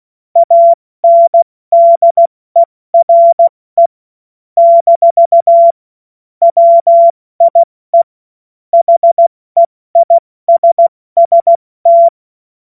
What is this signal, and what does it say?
Morse code, heard as a keyed tone, about 16 wpm, 680 Hz. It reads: ANDERE - WIE HEISST
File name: Morsekurs / Witze / Witz 26. April 2022 70 F80